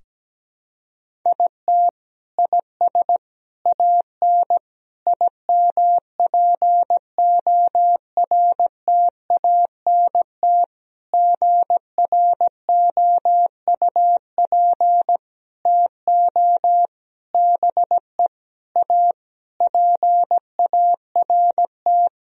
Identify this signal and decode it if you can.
Morse code, heard as a keyed tone, about 17 wpm, 700 Hz. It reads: IT IS AN IMPORTANT GROUP TO BE A PART
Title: Sentences from Top 1000 Words - Review Part 1 - 17wpm